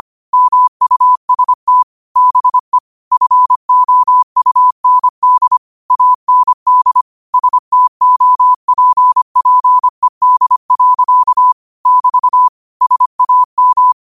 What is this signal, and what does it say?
Morse code, heard as a keyed tone, about 25 wpm, 1000 Hz. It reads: MUST BE FOUND AND STOPPED. <BT> SAM